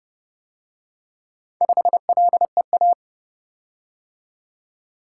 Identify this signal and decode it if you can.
Morse code, heard as a keyed tone, about 29 wpm, 700 Hz. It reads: 5LEA